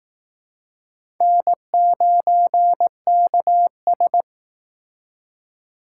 Morse code, heard as a keyed tone, about 18 wpm, 700 Hz. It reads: N9KS
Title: Speed Racing - USA 1x2 Call Signs 15wpm